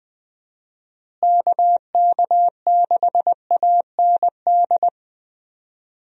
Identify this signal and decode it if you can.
Morse code, heard as a keyed tone, about 20 wpm, 700 Hz. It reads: KK6AND
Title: USA Call Signs - 6 Characters - Repeated 2x - 20wpm